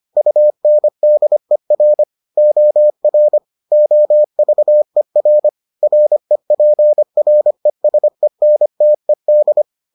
Morse code, heard as a keyed tone, about 25 wpm, 600 Hz. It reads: UNDER OR OVER REPRESENTED